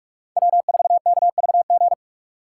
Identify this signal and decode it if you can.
Morse code, heard as a keyed tone, about 46 wpm, 700 Hz. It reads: W4KVG
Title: USA Call Signs - 2x Character Spacing - 45wpm